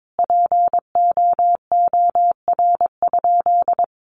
Morse code, heard as a keyed tone, about 22 wpm, 700 Hz. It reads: POOR?
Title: Sentences from Top 1100 Words 22wpm